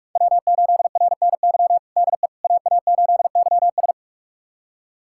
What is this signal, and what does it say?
Morse code, heard as a keyed tone, about 43 wpm, 700 Hz. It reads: W8RNY DE AA8YS
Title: USA Callsign DE Callsign 45wpm